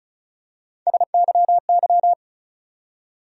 Morse code, heard as a keyed tone, about 34 wpm, 700 Hz. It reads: SYY
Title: USA Call Sign Postfixes - 3 Letters 35wpm